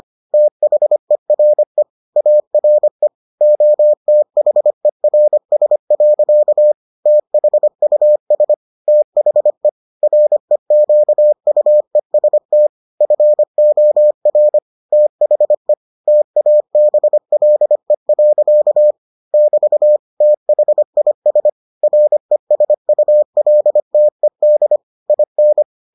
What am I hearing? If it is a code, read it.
Morse code, heard as a keyed tone, about 25 wpm, 600 Hz. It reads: THERE ARE OTHERS. THUS THE REQUEST FOR THE TABLE. <BT> THIS RESULTED IN